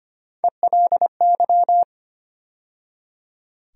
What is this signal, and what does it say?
Morse code, heard as a keyed tone, about 25 wpm, 700 Hz. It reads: ELY
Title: USA Call Sign Postfixes - 3 Letters 25wpm